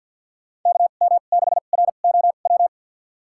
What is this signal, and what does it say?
Morse code, heard as a keyed tone, about 53 wpm, spaced as for 37 wpm, 700 Hz. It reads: KM6ROW
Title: USA Call Signs - 2x Character Spacing - 50wpm